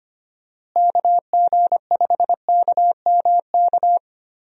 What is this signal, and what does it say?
Morse code, heard as a keyed tone, about 25 wpm, 700 Hz. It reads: KG5KMK
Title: USA Call Signs - 6 Characters - Repeated 2x - 25wpm